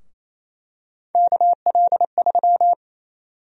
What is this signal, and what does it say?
Morse code, heard as a keyed tone, about 28 wpm, 700 Hz. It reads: KL3
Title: USA Call Sign Prefixes - 28wpm